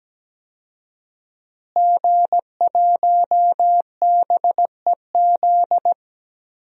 Morse code, heard as a keyed tone, about 17 wpm, 700 Hz. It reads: G1BEZ
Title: DXCC-100 Call Signs - 5 Characters - Repeated 2x - 17wpm